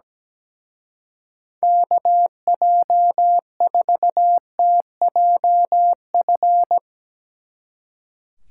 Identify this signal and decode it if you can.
Morse code, heard as a keyed tone, about 17 wpm, 700 Hz. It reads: KJ4TJF
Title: Speed Racing - USA Call Signs 17wpm